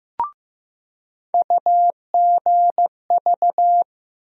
Morse code, heard as a keyed tone, about 15 wpm, 700 Hz. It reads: UGV
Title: USA Call Sign Postfixes - 3 Letters 15wpm